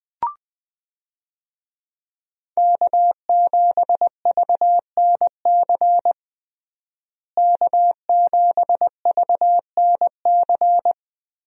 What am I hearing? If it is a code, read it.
Morse code, heard as a keyed tone, about 20 wpm, 700 Hz. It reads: K7VNC K7VNC
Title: USA Call Signs - 5 Characters - Repeated 3x - 20wpm